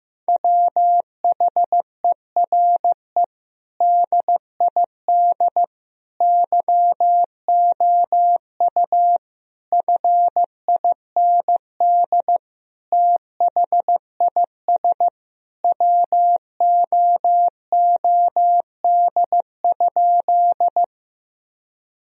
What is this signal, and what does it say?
Morse code, heard as a keyed tone, about 15 wpm, 700 Hz. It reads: WHERE DID YOU FIND THIS WOOD?
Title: Sentences from Top 1000 Words - Review Part 2 - 15wpm